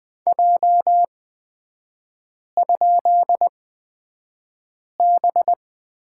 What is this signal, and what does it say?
Morse code, heard as a keyed tone, about 20 wpm, spaced as for 5 wpm, 690 Hz. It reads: J?B
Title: 229 Lesson 37e - Letter x - Three Characters 20wpm